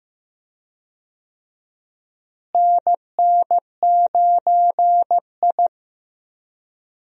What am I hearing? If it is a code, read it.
Morse code, heard as a keyed tone, about 15 wpm, 700 Hz. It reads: NN9I
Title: USA Call Signs - 4 Characters - Repeated 2x - 15wpm